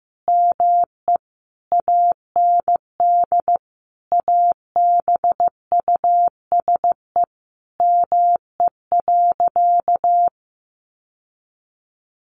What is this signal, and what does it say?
Morse code, heard as a keyed tone, about 15 wpm, 700 Hz. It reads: ME AND ABUSE ME.
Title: Sentences from Top 1300 Words 15wpm